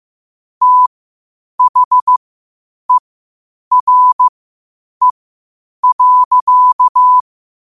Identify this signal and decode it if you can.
Morse code, heard as a keyed tone, about 15 wpm, spaced as for 8 wpm, 1000 Hz. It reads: THERE.
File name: qotdfiles / 8wpm